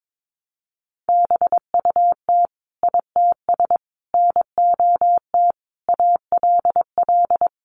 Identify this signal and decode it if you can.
Morse code, heard as a keyed tone, about 22 wpm, 700 Hz. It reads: BUT ITS NOT ALL
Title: Sentences from Top 600 Words 22wpm